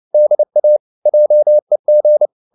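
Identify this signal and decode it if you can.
Morse code, heard as a keyed tone, about 29 wpm, 600 Hz. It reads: DA JEG
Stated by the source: Fast (QRQ) Morse at 29 wpm, from CW med Gnister.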